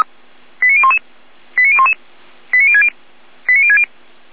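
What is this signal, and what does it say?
Five-tone selective call (ZVEI), heard as quick runs of five tones, tones of about 80 ms. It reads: 89011 89011 89060 89060
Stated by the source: ZVEI five-tone selective call received on an SDR.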